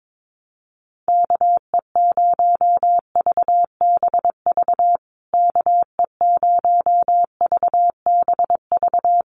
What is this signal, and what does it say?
Morse code, heard as a keyed tone, about 22 wpm, 700 Hz. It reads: KE0VBV KE0VBV
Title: USA Call Signs Repeated 3x 22wpm